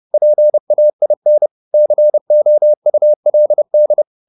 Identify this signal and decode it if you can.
Morse code, heard as a keyed tone, about 30 wpm, 600 Hz. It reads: PAIN COULD